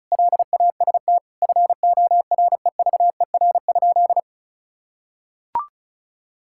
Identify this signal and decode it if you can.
Morse code, heard as a keyed tone, about 36 wpm, 700 Hz. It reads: LAST FOREVER?
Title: Sentences from Top 1600 Words 35wpm